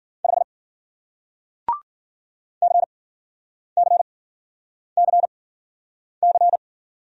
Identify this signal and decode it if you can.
Morse code, heard as a keyed tone, about 50 wpm, spaced as for 9 wpm, 700 Hz. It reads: 5 CCCC
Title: Speed-Racing Single Letter-Number 40wpm